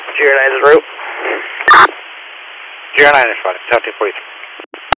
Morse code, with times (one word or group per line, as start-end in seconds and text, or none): none